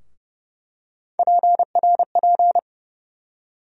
Morse code, heard as a keyed tone, about 29 wpm, 700 Hz. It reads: PRP
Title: USA Call Sign Postfixes - 3 Letters 30wpm